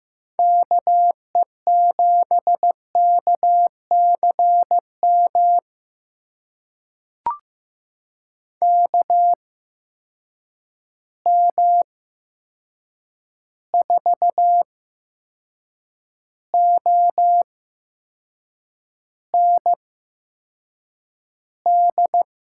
Morse code, heard as a keyed tone, about 15 wpm, spaced as for 6 wpm, 700 Hz. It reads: KE7KCM KM4OND
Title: USA Call Signs - 6 Characters - 8x Character Spacing - 15wpm